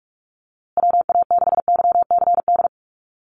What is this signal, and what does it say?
Morse code, heard as a keyed tone, about 44 wpm, 700 Hz. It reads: WA6YCD
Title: USA Call Signs Repeated 2x 45wpm